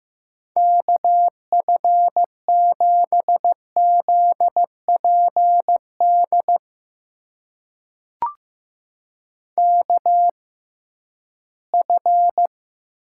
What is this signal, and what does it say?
Morse code, heard as a keyed tone, about 15 wpm, spaced as for 11 wpm, 700 Hz. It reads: KF7ZPD KF
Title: USA Call Signs - 6 Characters - 6x Character Spacing - 15wpm